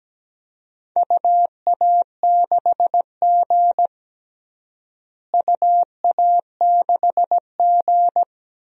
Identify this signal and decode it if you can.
Morse code, heard as a keyed tone, about 17 wpm, 700 Hz. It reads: UA6G UA6G
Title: DXCC-100 Call Signs - 4 Characters - Repeated 3x - 17wpm